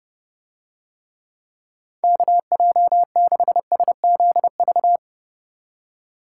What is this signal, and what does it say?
Morse code, heard as a keyed tone, about 30 wpm, 700 Hz. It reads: KJ6SZV